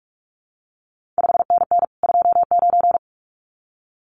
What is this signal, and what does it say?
Morse code, heard as a keyed tone, about 46 wpm, 700 Hz. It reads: 5NN 29
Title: CQ WW DX Contest Exchange Encore - 45wpm